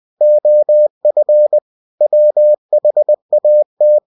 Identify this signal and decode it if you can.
Morse code, heard as a keyed tone, about 20 wpm, 600 Hz. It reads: OF WHAT
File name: cw-books / winnie-the-pooh / winnie-20wpm-page-0107